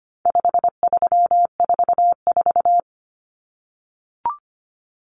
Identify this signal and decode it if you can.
Morse code, heard as a keyed tone, about 25 wpm, 700 Hz. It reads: 5344
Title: Numbers - 4-Digit 25wpm